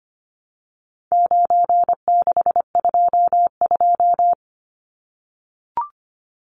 Morse code, heard as a keyed tone, about 25 wpm, 700 Hz. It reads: 9622
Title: Numbers - 4-Digit 25wpm